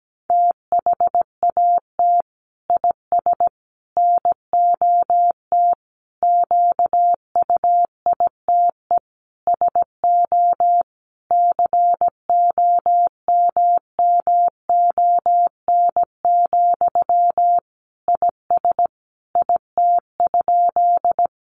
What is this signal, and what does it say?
Morse code, heard as a keyed tone, about 17 wpm, 700 Hz. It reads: THAT IS NOT QUITE SO COMMON, IS IT?
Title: Sentences from Top 700 Words 17wpm